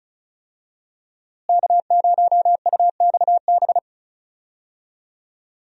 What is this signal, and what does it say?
Morse code, heard as a keyed tone, about 36 wpm, 700 Hz. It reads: K0UXB
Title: USA Call Signs - 5 Characters - Repeated 2x - 35wpm